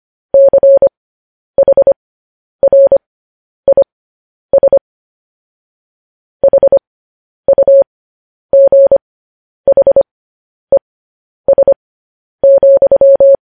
Morse code, heard as a keyed tone, about 25 wpm, spaced as for 10 wpm, 550 Hz. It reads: CHRIS HUGHES,
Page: news headlines in morse code at 10 wpm